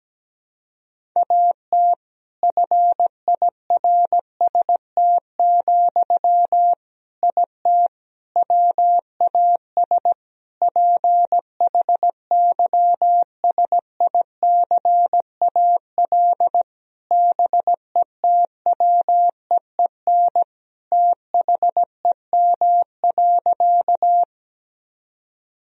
Morse code, heard as a keyed tone, about 17 wpm, 700 Hz. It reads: AT FIRST, IT WAS PHYSICAL BETWEEN THEM.
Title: Sentences from Top 1000 Words - Review Part 4 - 17wpm